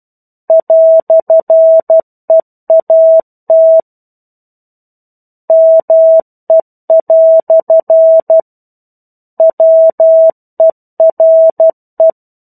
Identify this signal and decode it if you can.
Morse code, heard as a keyed tone, about 12 wpm, 650 Hz. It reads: "EAT ME" WERE
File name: 01-Chapter-01 12 WPM